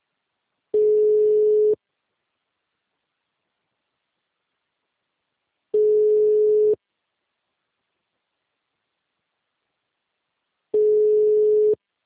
Ringback tone (Europe, 425 Hz).